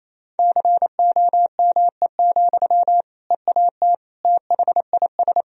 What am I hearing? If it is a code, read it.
Morse code, heard as a keyed tone, about 28 wpm, 700 Hz. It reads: COME, EAT THIS
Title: Sentences from Top 1000 Words - Review Part 1 - 28wpm